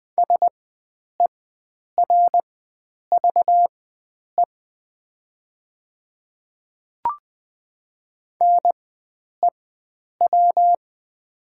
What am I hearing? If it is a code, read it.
Morse code, heard as a keyed tone, about 20 wpm, spaced as for 9 wpm, 700 Hz. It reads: SERVE NEW
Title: 4x Character Spacing - Top 500 Words - 20wpm